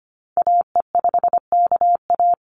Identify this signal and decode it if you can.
Morse code, heard as a keyed tone, about 25 wpm, 700 Hz. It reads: AE5KA
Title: NA SOTA Activator Callsigns 25wpm